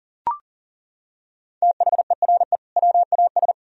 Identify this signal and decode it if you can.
Morse code, heard as a keyed tone, about 40 wpm, 700 Hz. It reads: THERE WAS
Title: Sentences from Top 1000 Words - Review Part 3 - 40wpm